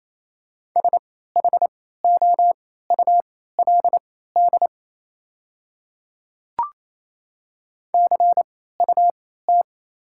Morse code, heard as a keyed tone, about 28 wpm, spaced as for 16 wpm, 700 Hz. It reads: SHOULD CUT
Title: 3x Character Spacing - Top 500 Words - 28wpm